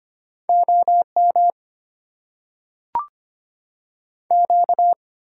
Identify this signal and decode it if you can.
Morse code, heard as a keyed tone, about 25 wpm, 700 Hz. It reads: OM Q
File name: Top 90 QSO Elements - 6x Character Spacing - 25wpm